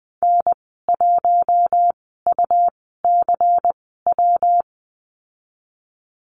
Morse code, heard as a keyed tone, about 20 wpm, spaced as for 15 wpm, 700 Hz. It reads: N1UCW
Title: 161 Lesson 26g - Letter g - Call Signs 20wpm